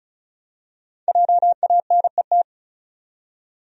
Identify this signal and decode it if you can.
Morse code, heard as a keyed tone, about 34 wpm, 700 Hz. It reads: JANET
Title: USA Common Names - 35wpm